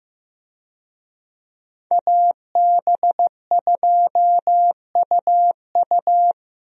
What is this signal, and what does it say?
Morse code, heard as a keyed tone, about 16 wpm, 700 Hz. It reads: AB2UU
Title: USA Call Signs - 5 Characters - Repeated 3x - 15wpm